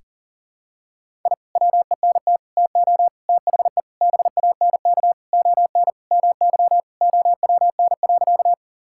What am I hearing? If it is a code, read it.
Morse code, heard as a keyed tone, about 40 wpm, 700 Hz. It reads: I WENT TO THE BANK ON MY OWN.
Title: Sentences from Top 1000 Words - Review Part 6 - 40wpm